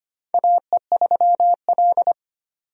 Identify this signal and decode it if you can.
Morse code, heard as a keyed tone, about 25 wpm, 700 Hz. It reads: AE3L
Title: USA Call Signs - 4 Characters - 25wpm